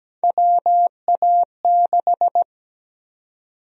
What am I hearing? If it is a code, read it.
Morse code, heard as a keyed tone, about 17 wpm, 700 Hz. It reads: WA6
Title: USA Call Sign Prefixes - 17wpm